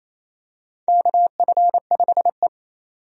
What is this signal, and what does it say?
Morse code, heard as a keyed tone, about 28 wpm, 700 Hz. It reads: KF5E
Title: USA Call Signs - 4 Characters - 28wpm